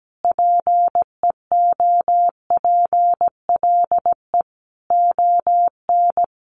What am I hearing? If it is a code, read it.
Morse code, heard as a keyed tone, about 17 wpm, 700 Hz. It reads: PEOPLE ON
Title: Sentences from Top 1100 Words 17wpm